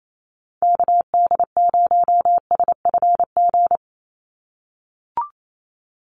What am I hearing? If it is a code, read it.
Morse code, heard as a keyed tone, about 28 wpm, 700 Hz. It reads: KD0SFG
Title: USA Call Signs 28wpm